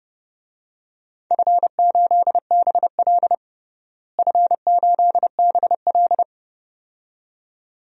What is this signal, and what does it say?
Morse code, heard as a keyed tone, about 30 wpm, 700 Hz. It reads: F8BL F8BL
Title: DXCC-100 Call Signs - 4 Characters - Repeated 2x - 30wpm